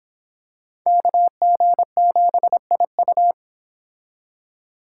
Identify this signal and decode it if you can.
Morse code, heard as a keyed tone, about 25 wpm, 700 Hz. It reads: KG7IU